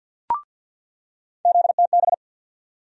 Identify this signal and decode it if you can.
Morse code, heard as a keyed tone, about 51 wpm, 700 Hz. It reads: ZTB
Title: USA Call Sign Postfixes - 3 Letters 50wpm